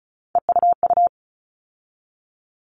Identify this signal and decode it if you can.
Morse code, heard as a keyed tone, about 34 wpm, 700 Hz. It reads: EUU
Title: All Cut Numbers - 3-Digit 35wpm